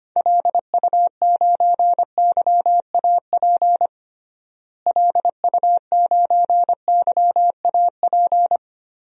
Morse code, heard as a keyed tone, about 25 wpm, 700 Hz. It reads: LU9YAP LU9YAP
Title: DXCC-100 Call Signs - 6 Characters - Repeated 3x - 25wpm